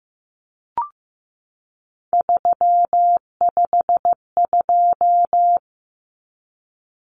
Morse code, heard as a keyed tone, about 15 wpm, 700 Hz. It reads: 352
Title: Numbers - 3-Digit 15wpm